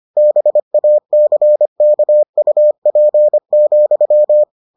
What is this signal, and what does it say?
Morse code, heard as a keyed tone, about 25 wpm, 600 Hz. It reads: BACKUP,